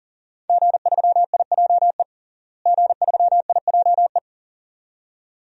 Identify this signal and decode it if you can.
Morse code, heard as a keyed tone, about 37 wpm, 700 Hz. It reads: G3IJE G3IJE
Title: DXCC-100 Call Signs - 5 Characters - Repeated 2x - 40wpm